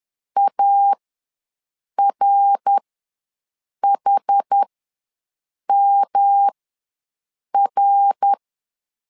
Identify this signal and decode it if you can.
Morse code, heard as a keyed tone, about 11 wpm, spaced as for 6 wpm, 800 Hz. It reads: ARHMR
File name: LEZIONI / 3 MWRO / 3 MWR ws 10wpm 2151